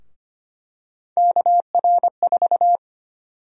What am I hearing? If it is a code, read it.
Morse code, heard as a keyed tone, about 25 wpm, 700 Hz. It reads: KR4
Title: USA Call Sign Prefixes - 25wpm